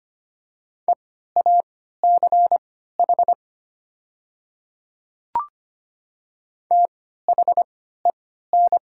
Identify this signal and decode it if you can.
Morse code, heard as a keyed tone, about 25 wpm, spaced as for 14 wpm, 700 Hz. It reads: EACH THEN
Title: 3x Character Spacing - Top 100 Words - 25wpm